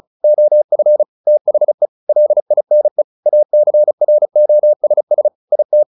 Morse code, heard as a keyed tone, about 34 wpm, 600 Hz. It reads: OF THE LINE ACROSS IT
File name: the old man and the sea 2 35wpm